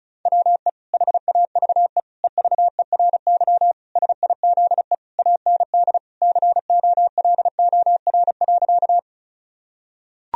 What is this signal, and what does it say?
Morse code, heard as a keyed tone, about 34 wpm, 700 Hz. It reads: WE HAVE EVERY SIZE AND COLOR.
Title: Sentences from Top 1000 Words - Review Part 2 - 35wpm